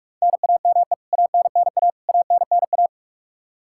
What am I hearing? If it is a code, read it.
Morse code, heard as a keyed tone, about 46 wpm, 700 Hz. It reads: NAME ANNA ANNA
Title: K1USN SST QSO Practice 45wpm